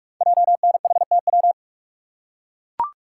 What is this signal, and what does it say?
Morse code, heard as a keyed tone, about 44 wpm, 700 Hz. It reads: JNHTW
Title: Code Groups - 5 Letters - 45wpm